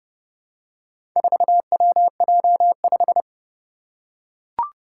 Morse code, heard as a keyed tone, about 29 wpm, 700 Hz. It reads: 4WJ5
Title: Four Challenging Letter-Number Combinations 30wpm